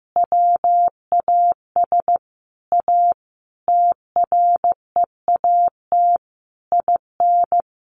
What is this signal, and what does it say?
Morse code, heard as a keyed tone, about 15 wpm, 700 Hz. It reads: WAS A TREAT IN
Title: Sentences from Top 1300 Words 15wpm